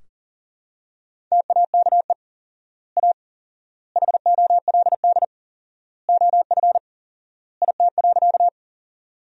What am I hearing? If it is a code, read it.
Morse code, heard as a keyed tone, about 40 wpm, 700 Hz. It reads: TAKE A HOLD OF IT.
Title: 4x Word-Spacing - Sentences from Top 400 Words - 40wpm